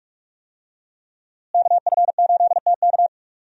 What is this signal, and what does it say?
Morse code, heard as a keyed tone, about 41 wpm, 700 Hz. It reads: KF8TK
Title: USA Call Signs - 5 Characters - Repeated 3x - 45wpm